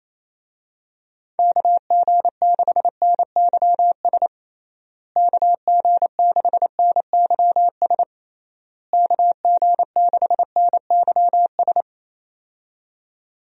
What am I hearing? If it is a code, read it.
Morse code, heard as a keyed tone, about 28 wpm, 700 Hz. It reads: KG6NYS KG6NYS KG6NYS
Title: USA Call Signs - 6 Characters - Repeated 3x - 28wpm